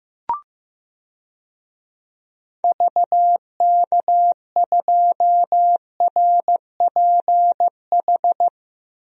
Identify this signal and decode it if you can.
Morse code, heard as a keyed tone, about 15 wpm, 700 Hz. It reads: VK2RPH
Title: DXCC-100 Call Signs - 6 Characters - 15wpm